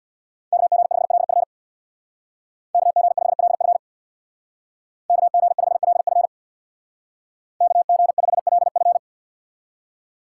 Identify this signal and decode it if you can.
Morse code, heard as a keyed tone, about 64 wpm, 700 Hz. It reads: KG5LF KG5LF KG5LF KG5LF